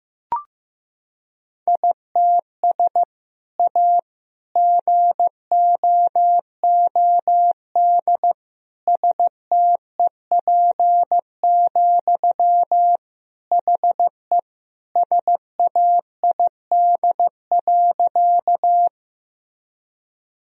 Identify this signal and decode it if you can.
Morse code, heard as a keyed tone, about 15 wpm, 700 Hz. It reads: ITS A GOOD STEP, HE SAID.
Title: Sentences from Top 1000 Words - Review Part 2 - 15wpm